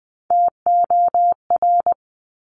Morse code, heard as a keyed tone, about 20 wpm, 700 Hz. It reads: TOR
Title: Top Three Letter Combinations in Words 20wpm